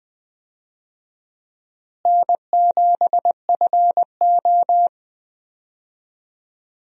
Morse code, heard as a keyed tone, about 20 wpm, 700 Hz. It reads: N7FO
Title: USA Call Signs - 4 Characters - Repeated 3x - 20wpm